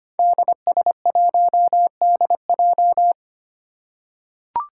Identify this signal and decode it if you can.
Morse code, heard as a keyed tone, about 25 wpm, 700 Hz. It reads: DS1DJ
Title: Five Challenging Letter-Number Combinations 25wpm